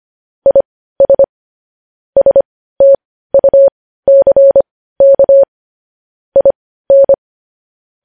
Morse code, heard as a keyed tone, about 25 wpm, spaced as for 15 wpm, 550 Hz. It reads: IS STUCK IN
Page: news headlines in morse code at 15 wpm